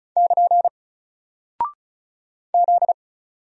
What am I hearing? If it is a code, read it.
Morse code, heard as a keyed tone, about 36 wpm, 700 Hz. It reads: <KN> Z
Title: Single Character (Letter:Number:Punctuation:Prosign) Alt 35wpm